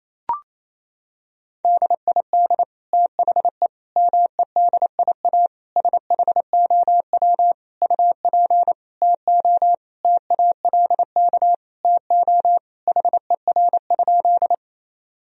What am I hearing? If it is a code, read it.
Morse code, heard as a keyed tone, about 28 wpm, 700 Hz. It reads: DID THE MEDIA SHOW UP TO TALK TO HER?
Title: Sentences from Top 1000 Words - Review Part 4 - 28wpm